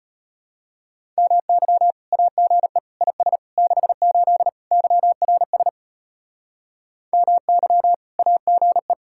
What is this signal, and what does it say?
Morse code, heard as a keyed tone, about 31 wpm, 700 Hz. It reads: MY AGE IS 68 YRS MY AGE
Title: Speed-Racing Common QSO Phrases - 25wpm